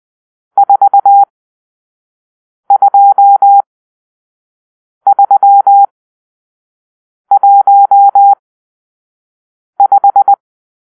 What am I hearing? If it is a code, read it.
Morse code, heard as a keyed tone, about 20 wpm, spaced as for 5 wpm, 800 Hz. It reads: 42315
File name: arrl / Grp-6 Drill-2